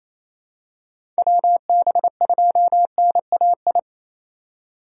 Morse code, heard as a keyed tone, about 28 wpm, 700 Hz. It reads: WB2NAI
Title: USA Call Signs - 6 Characters - 28wpm